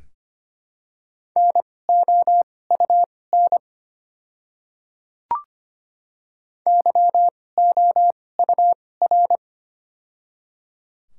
Morse code, heard as a keyed tone, about 25 wpm, spaced as for 18 wpm, 700 Hz. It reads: NOUN YOUR
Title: 2x Character Spacing - Top 500 Words - 25wpm